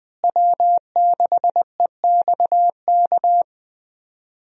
Morse code, heard as a keyed tone, about 20 wpm, 700 Hz. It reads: W6EXK
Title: USA Call Signs - 5 Characters - Repeated 2x - 20wpm